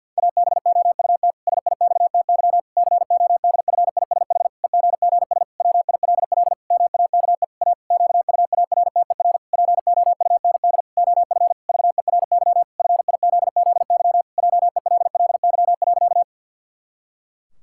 Morse code, heard as a keyed tone, about 48 wpm, 700 Hz. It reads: ABOUT SIXTY CODFISH EGGS WILL MAKE A QUARTER POUND OF VERY FIZZY JELLY.
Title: Pangram Challenge v2 - 50wpm